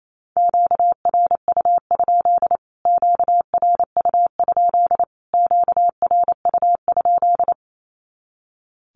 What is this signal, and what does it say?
Morse code, heard as a keyed tone, about 28 wpm, 700 Hz. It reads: QRU? QRU? QRU?
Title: Q-Codes Repeated 3x 28wpm